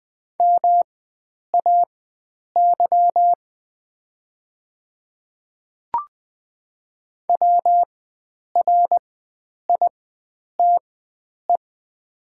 Morse code, heard as a keyed tone, about 20 wpm, spaced as for 9 wpm, 700 Hz. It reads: MAY WRITE
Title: 4x Character Spacing - Top 100 Words - 20wpm